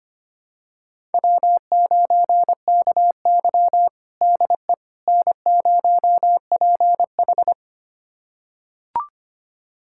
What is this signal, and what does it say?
Morse code, heard as a keyed tone, about 25 wpm, 700 Hz. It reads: W9KY DE N0PH
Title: USA Callsign DE Callsign - 4 Characters - 25wpm